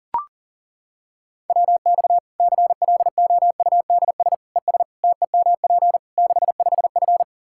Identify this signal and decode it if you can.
Morse code, heard as a keyed tone, about 40 wpm, 700 Hz. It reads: WX CLOUDS ES TEMP 65F
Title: Practice QSO - Exchange 2 - Info - 40wpm